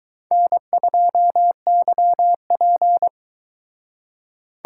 Morse code, heard as a keyed tone, about 23 wpm, 700 Hz. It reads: N2YP